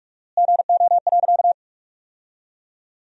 Morse code, heard as a keyed tone, about 45 wpm, 700 Hz. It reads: GO.
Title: 4x Word-Spacing - Sentences from Top 300 Words - 45wpm